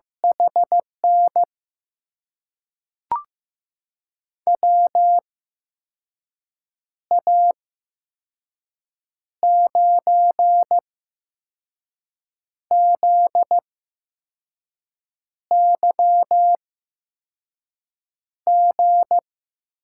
Morse code, heard as a keyed tone, about 15 wpm, spaced as for 5 wpm, 700 Hz. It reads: HN WA9ZYG
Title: USA Call Signs - 6 Characters - 8x Character Spacing - 15wpm